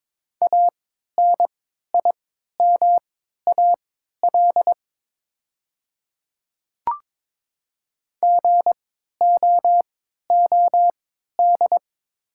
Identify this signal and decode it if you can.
Morse code, heard as a keyed tone, about 23 wpm, spaced as for 13 wpm, 690 Hz. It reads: ANIMAL GOOD
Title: 3x Character Spacing - Top 300 Words - 22wpm